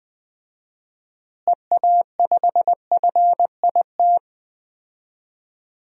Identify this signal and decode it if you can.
Morse code, heard as a keyed tone, about 21 wpm, 700 Hz. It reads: EA5FIT
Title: DXCC-100 Call Signs - 6 Characters - Repeated 3x - 20wpm